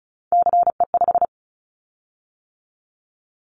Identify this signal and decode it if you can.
Morse code, heard as a keyed tone, about 36 wpm, 700 Hz. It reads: CE5
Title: Callsign Prefixes for Top 100 DXCC - 35wpm